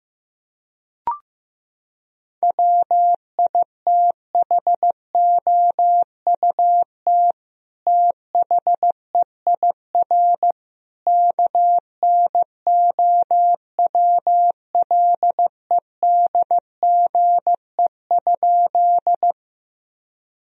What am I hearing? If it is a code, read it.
Morse code, heard as a keyed tone, about 15 wpm, 700 Hz. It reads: WITHOUT THEIR KNOWLEDGE?
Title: Sentences from Top 1000 Words - Review Part 3 - 15wpm